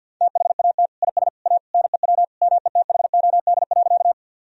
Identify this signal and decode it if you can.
Morse code, heard as a keyed tone, about 50 wpm, 700 Hz. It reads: THAT IS A NEW METHOD.
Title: Sentences from Top 1000 Words - Review Part 3 - 50wpm